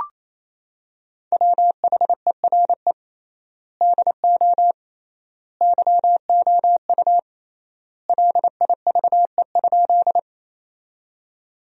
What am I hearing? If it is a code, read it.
Morse code, heard as a keyed tone, about 28 wpm, 700 Hz. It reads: WHERE DO YOU LIVE?
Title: 3x Word-Spacing - Sentences from Top 200 Words - 28wpm